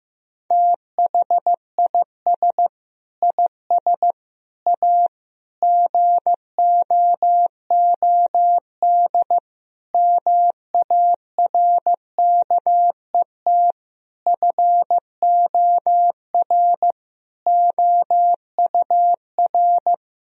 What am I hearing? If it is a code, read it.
Morse code, heard as a keyed tone, about 15 wpm, 700 Hz. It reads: THIS IS A GOOD MARKET FOR OUR
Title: Sentences from Top 1000 Words - Review Part 3 - 15wpm